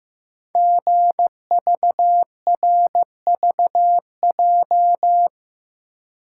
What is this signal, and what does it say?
Morse code, heard as a keyed tone, about 15 wpm, 700 Hz. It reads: GVRVJ
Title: Code Groups - 5 Letter:Numbers - 15wpm